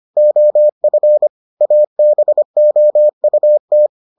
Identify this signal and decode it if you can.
Morse code, heard as a keyed tone, about 25 wpm, 600 Hz. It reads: OF ABOUT